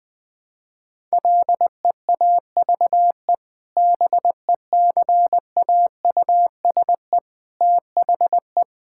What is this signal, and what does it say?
Morse code, heard as a keyed tone, about 20 wpm, 700 Hz. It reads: LEAVE BECAUSE THE
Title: Sentences with 2-Word Phrases - 20wpm